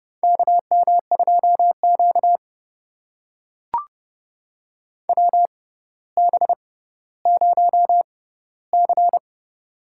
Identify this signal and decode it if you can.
Morse code, heard as a keyed tone, about 30 wpm, spaced as for 15 wpm, 690 Hz. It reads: KM2Q WB0C
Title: USA Call Signs - 4 Characters - 6x Character Spacing - 30wpm